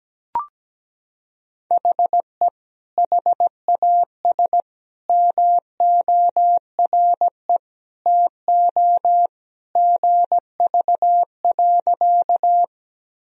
Morse code, heard as a keyed tone, about 17 wpm, 700 Hz. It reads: HE HAS MORE TO GV.
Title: Sentences from Top 200 Words and Abbreviations 17wpm